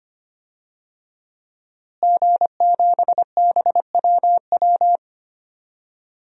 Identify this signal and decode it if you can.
Morse code, heard as a keyed tone, about 25 wpm, 700 Hz. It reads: G7BWW
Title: DXCC-100 Call Signs - 5 Characters - Repeated 2x - 25wpm